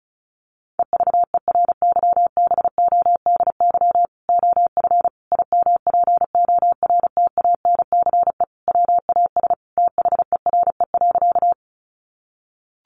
Morse code, heard as a keyed tone, about 34 wpm, 700 Hz. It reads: EVERYBODY OF IMPORTANCE WAS THERE.
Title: Sentences from Top 1400 Words 35wpm